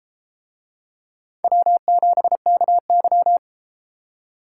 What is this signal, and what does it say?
Morse code, heard as a keyed tone, about 33 wpm, 700 Hz. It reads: W7KY